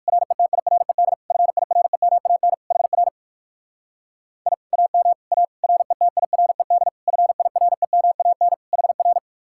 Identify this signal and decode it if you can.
Morse code, heard as a keyed tone, about 55 wpm, 700 Hz. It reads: RETIRED FIREMAN HR I AM A RETIRED FIREMAN HR
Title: Speed-Racing Common QSO Phrases - 45wpm